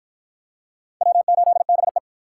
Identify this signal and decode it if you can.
Morse code, heard as a keyed tone, about 53 wpm, 700 Hz. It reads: W8BE